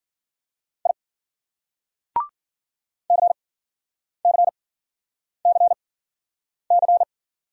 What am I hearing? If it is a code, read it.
Morse code, heard as a keyed tone, about 56 wpm, spaced as for 9 wpm, 700 Hz. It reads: I CCCC